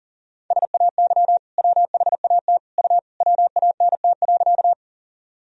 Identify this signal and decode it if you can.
Morse code, heard as a keyed tone, about 40 wpm, 700 Hz. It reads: SAY WHAT U WANT.
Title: Sentences from Top 200 Words and Abbreviations 40wpm